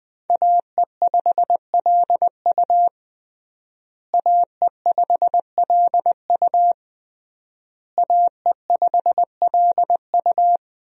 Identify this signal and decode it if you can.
Morse code, heard as a keyed tone, about 20 wpm, 700 Hz. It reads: AE5LU AE5LU AE5LU
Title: USA Call Signs - 5 Characters - Repeated 3x - 20wpm